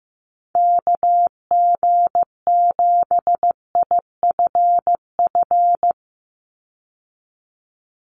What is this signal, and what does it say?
Morse code, heard as a keyed tone, about 15 wpm, 700 Hz. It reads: KG7IFF
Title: USA Call Signs 15wpm